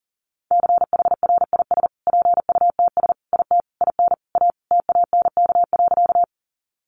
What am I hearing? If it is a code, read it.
Morse code, heard as a keyed tone, about 40 wpm, 700 Hz. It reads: CHRIS PUTS IT IN A TANK.
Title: Easy Phrases 40wpm